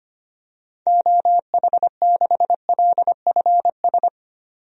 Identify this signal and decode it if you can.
Morse code, heard as a keyed tone, about 25 wpm, 700 Hz. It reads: OH6LFS
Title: DXCC-100 Call Signs - 6 Characters - 25wpm